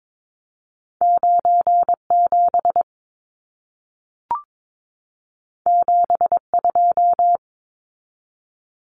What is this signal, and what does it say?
Morse code, heard as a keyed tone, about 22 wpm, 700 Hz. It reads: 97 72
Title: Numbers - 2-Digit 22wpm